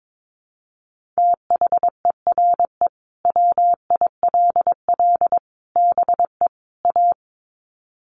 Morse code, heard as a keyed tone, about 22 wpm, 700 Hz. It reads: THERE WILL BE A
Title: Sets of 4 Words 22wpm